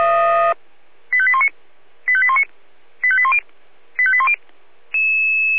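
Five-tone selective call (ZVEI), heard as quick runs of five tones, tones of about 70 ms. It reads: 85819 85819 85810 85810
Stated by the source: ZVEI five-tone selective call 85880, received on SDR.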